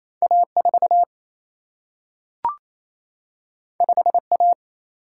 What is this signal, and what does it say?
Morse code, heard as a keyed tone, about 28 wpm, 700 Hz. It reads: A4 5A
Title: Single-Letter Single-Number 28wpm